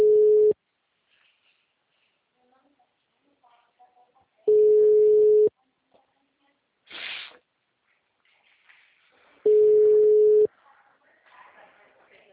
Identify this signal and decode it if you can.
Ringback tone (Europe, 425 Hz), with speech.
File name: out- -2017- - - .8811